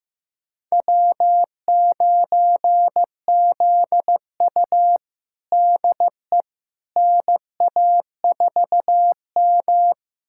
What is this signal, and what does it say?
Morse code, heard as a keyed tone, about 15 wpm, 700 Hz. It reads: W9ZU DE NA4M
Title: USA Callsign DE Callsign - 4 Characters - 15wpm